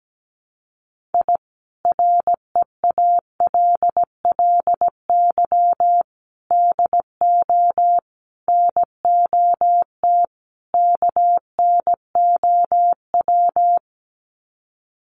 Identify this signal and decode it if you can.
Morse code, heard as a keyed tone, about 17 wpm, 700 Hz. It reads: I REALLY DO NOT KNOW
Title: Sets of 5 Words Encore 17wpm